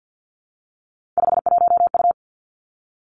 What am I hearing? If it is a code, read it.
Morse code, heard as a keyed tone, about 50 wpm, 700 Hz. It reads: 51U
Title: Three Challenging Letter-Number Combinations 50wpm